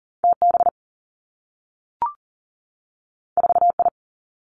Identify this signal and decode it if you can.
Morse code, heard as a keyed tone, about 38 wpm, 700 Hz. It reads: TB 4I
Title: Two Challenging Letter-Number Combinations 40wpm